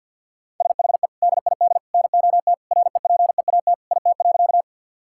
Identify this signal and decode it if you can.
Morse code, heard as a keyed tone, about 50 wpm, 700 Hz. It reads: SHE DID NOT REPEAT IT.
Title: Sentences from Top 1600 Words 50wpm